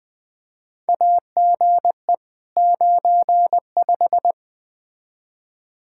Morse code, heard as a keyed tone, about 20 wpm, 700 Hz. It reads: AGE 95
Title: K1USN SST QSO Practice 20wpm